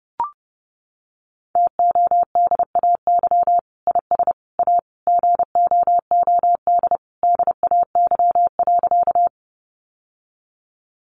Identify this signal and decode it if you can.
Morse code, heard as a keyed tone, about 30 wpm, 700 Hz. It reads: TODAY IS A GOOD DAY.
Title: Sentences from Top 600 Words 30wpm